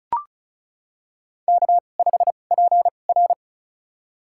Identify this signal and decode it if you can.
Morse code, heard as a keyed tone, about 34 wpm, spaced as for 25 wpm, 700 Hz. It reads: K5PR
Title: USA Call Signs - 2x Character Spacing - 35wpm